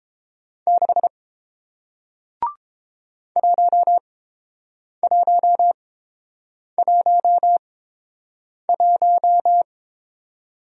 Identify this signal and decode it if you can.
Morse code, heard as a keyed tone, about 30 wpm, spaced as for 7 wpm, 700 Hz. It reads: 6 1111